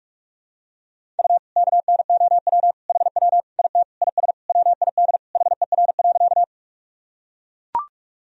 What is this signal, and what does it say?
Morse code, heard as a keyed tone, about 42 wpm, 700 Hz. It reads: U KNOW HW IT IS WID HER.
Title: Sentences from Top 100 Words and Abbreviations 45wpm